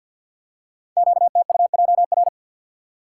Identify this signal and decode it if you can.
Morse code, heard as a keyed tone, about 48 wpm, 700 Hz. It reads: QTUJR
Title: Code Groups - 5 Letters - 50wpm